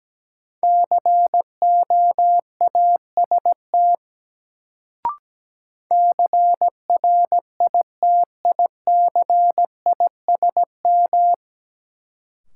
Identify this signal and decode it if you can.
Morse code, heard as a keyed tone, about 17 wpm, 700 Hz. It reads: COAST CRITICISM
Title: Sentences from Top 1300 Words - New Word Drill Alt - 17wpm